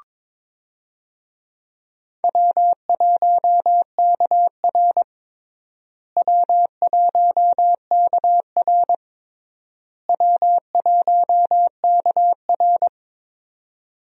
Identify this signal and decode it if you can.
Morse code, heard as a keyed tone, about 22 wpm, 700 Hz. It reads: W1KR W1KR W1KR
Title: USA Call Signs - 4 Characters - Repeated 3x - 22wpm